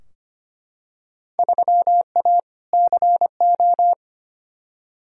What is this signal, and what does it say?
Morse code, heard as a keyed tone, about 25 wpm, 700 Hz. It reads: 3A CO
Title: Field Day Exchanges Encore - 25wpm